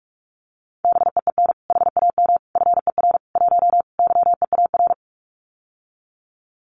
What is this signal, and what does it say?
Morse code, heard as a keyed tone, about 46 wpm, 700 Hz. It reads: BEEN HAM FER 1 YEAR
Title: Common QSO Phrases Encore 45wpm